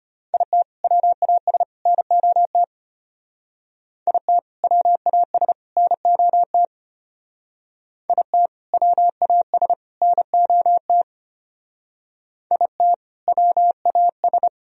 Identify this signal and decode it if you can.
Morse code, heard as a keyed tone, about 32 wpm, 700 Hz. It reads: IT WAS NOT IT WAS NOT IT WAS NOT IT WAS